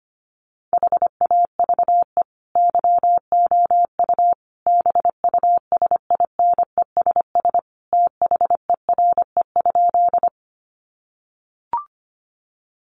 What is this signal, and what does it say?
Morse code, heard as a keyed tone, about 25 wpm, 700 Hz. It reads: HAVE YOU BUSINESS THERE?
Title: Sentences from Top 600 Words 25wpm